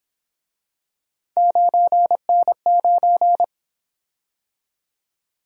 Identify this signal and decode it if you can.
Morse code, heard as a keyed tone, about 27 wpm, 700 Hz. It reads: 9N9
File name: Speed-Racing - Three Letter-Number Combinations - 22wpm